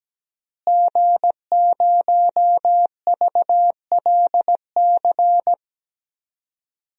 Morse code, heard as a keyed tone, about 17 wpm, 700 Hz. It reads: G0VLC
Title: DXCC-100 Call Signs - 5 Characters - Repeated 3x - 17wpm